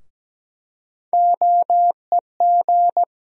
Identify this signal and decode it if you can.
Morse code, heard as a keyed tone, about 17 wpm, 700 Hz. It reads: OEG
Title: USA Call Sign Postfixes - 3 Letters 17wpm